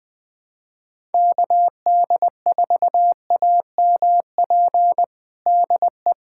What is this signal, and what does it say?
Morse code, heard as a keyed tone, about 21 wpm, 700 Hz. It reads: KD4AMP DE
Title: USA Callsign DE Callsign 20wpm